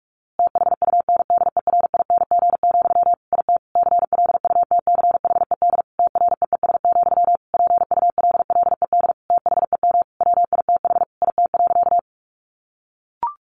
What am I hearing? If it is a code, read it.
Morse code, heard as a keyed tone, about 44 wpm, 700 Hz. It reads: THUNDERING, IT CLUTCHED TREES, PULLED THEM WITH IT.
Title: A Sound Of Thunder 45wpm